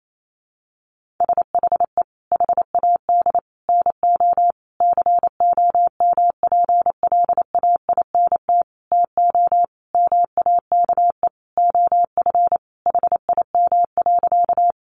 Morse code, heard as a keyed tone, about 28 wpm, 700 Hz. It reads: SHE HAD NO COMPLAINT TO MAKE OF HIM.
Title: Sentences from Top 1400 Words 28wpm